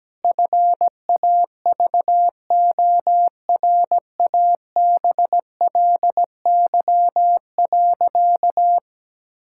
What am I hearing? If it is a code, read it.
Morse code, heard as a keyed tone, about 17 wpm, 700 Hz. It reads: FAVORABLY.
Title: Pangram Challenge v2 - 17wpm